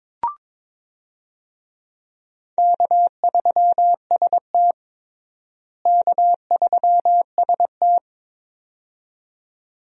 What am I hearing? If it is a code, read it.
Morse code, heard as a keyed tone, about 22 wpm, 700 Hz. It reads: K3ST K3ST
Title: USA Call Signs - 4 Characters - Repeated 2x - 22wpm